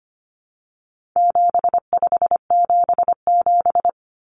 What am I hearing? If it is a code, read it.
Morse code, heard as a keyed tone, about 25 wpm, 700 Hz. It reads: 7577
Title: Common Cut Numbers - 4-Digit 25wpm